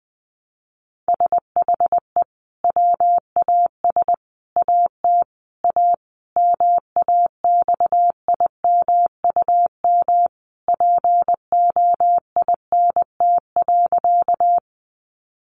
Morse code, heard as a keyed tone, about 20 wpm, 700 Hz. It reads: SHE WAS AT A MAXIMUM POINT.
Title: Sentences from Top 1300 Words 20wpm